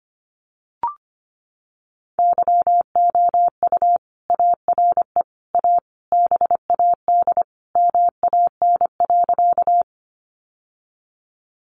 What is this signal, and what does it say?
Morse code, heard as a keyed tone, about 25 wpm, 700 Hz. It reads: YOU ARE A BAD MAN.
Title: Sentences from Top 700 Words 25wpm